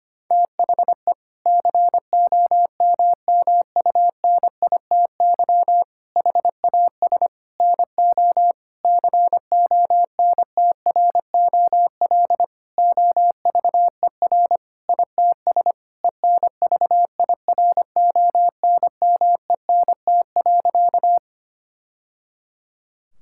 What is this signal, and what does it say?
Morse code, heard as a keyed tone, about 25 wpm, 700 Hz. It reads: THE COMMUNITY HAS NO CONTROL OVER ITS ENVIRONMENT.
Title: Sentences from Top 1000 Words - Review Part 5 - 25wpm